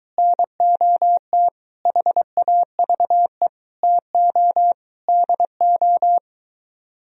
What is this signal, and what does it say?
Morse code, heard as a keyed tone, about 24 wpm, 690 Hz. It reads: NOT HAVE TO DO